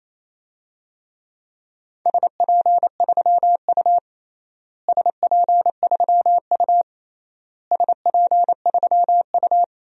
Morse code, heard as a keyed tone, about 28 wpm, 700 Hz. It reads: SP3U SP3U SP3U
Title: DXCC-100 Call Signs - 4 Characters - Repeated 3x - 28wpm